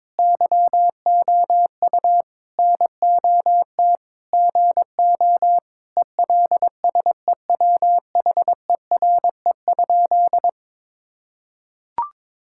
Morse code, heard as a keyed tone, about 22 wpm, 700 Hz. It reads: YOU NOT GO ELSEWHERE?
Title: Sentences from Top 1600 Words 22wpm